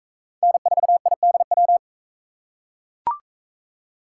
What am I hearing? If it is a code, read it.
Morse code, heard as a keyed tone, about 37 wpm, 700 Hz. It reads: N4IDW